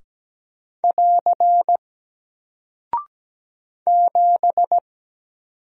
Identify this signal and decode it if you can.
Morse code, heard as a keyed tone, about 17 wpm, 700 Hz. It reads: <AR> 7
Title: Single Character (Letter:Number:Punctuation:Prosign) Alt 17wpm